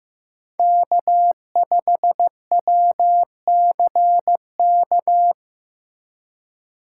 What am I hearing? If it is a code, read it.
Morse code, heard as a keyed tone, about 15 wpm, 700 Hz. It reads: K5WCK
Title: USA Call Signs - 5 Characters - 4x Character Spacing - 15wpm